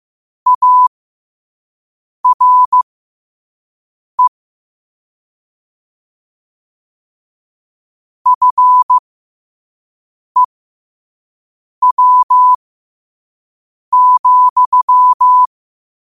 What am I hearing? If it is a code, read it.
Morse code, heard as a keyed tone, about 15 wpm, spaced as for 5 wpm, 1000 Hz. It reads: ARE FEW,
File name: qotdfiles / 5wpm